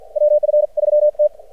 Morse code, heard as a keyed tone, about 50 wpm, 600 Hz. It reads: WF3T
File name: archived / CRSnet